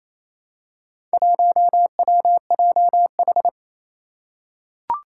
Morse code, heard as a keyed tone, about 28 wpm, 700 Hz. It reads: 1WJH
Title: Four Challenging Letter-Number Combinations 28wpm